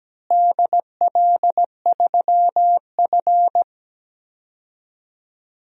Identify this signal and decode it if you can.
Morse code, heard as a keyed tone, about 17 wpm, 700 Hz. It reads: DL3F
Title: DXCC-100 Call Signs - 4 Characters - Repeated 3x - 17wpm